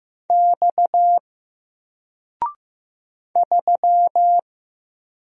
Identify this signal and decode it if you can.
Morse code, heard as a keyed tone, about 15 wpm, 700 Hz. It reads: X 3
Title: Single Letter-Number 15wpm